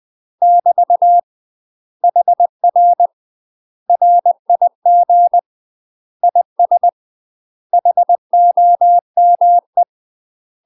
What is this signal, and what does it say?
Morse code, heard as a keyed tone, about 20 wpm, 700 Hz. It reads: <BT> HR RIG IS HOME
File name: UK MASTER QSOs TRAINING / 20WPM / UK MASTER QSOs 0074